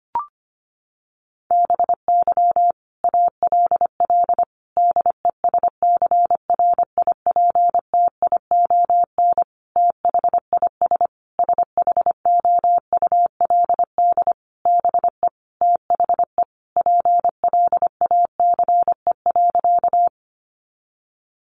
Morse code, heard as a keyed tone, about 25 wpm, 700 Hz. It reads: BY ALL DESCRIPTION THIS SHOULD BE THE PLACE.
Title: Sentences from Top 1300 Words 25wpm